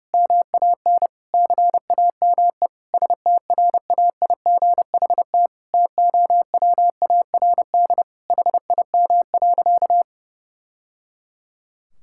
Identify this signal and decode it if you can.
Morse code, heard as a keyed tone, about 32 wpm, 700 Hz. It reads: MAN CAME STRAIGHT TOWARD HIM.
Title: Sentences from Top 1000 Words - Review Part 6 - 30wpm